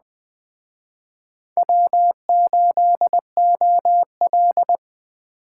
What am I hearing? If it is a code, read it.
Morse code, heard as a keyed tone, about 20 wpm, 700 Hz. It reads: W8OL
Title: Speed Racing - USA Call Signs 17wpm